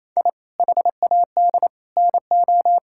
Morse code, heard as a keyed tone, about 27 wpm, 700 Hz. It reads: I HAD NO
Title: Sentences from Top 1000 Words - Review Part 6 - 28wpm